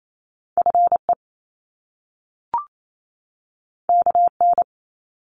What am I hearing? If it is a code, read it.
Morse code, heard as a keyed tone, about 28 wpm, 700 Hz. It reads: FE KN
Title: Top Two Letter Combinations in Words 28wpm